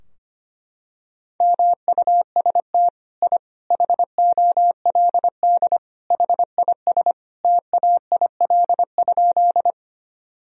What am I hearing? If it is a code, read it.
Morse code, heard as a keyed tone, about 25 wpm, 700 Hz. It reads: MUST I HOLD HIS TAIL?
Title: Sentences from Top 1000 Words - Review Part 3 - 25wpm